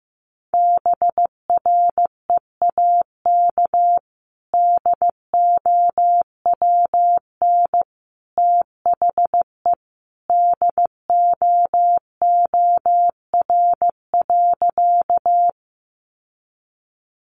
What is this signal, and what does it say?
Morse code, heard as a keyed tone, about 15 wpm, 700 Hz. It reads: BREAK DOWN THE DOOR.
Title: Sentences from Top 900 Words 15wpm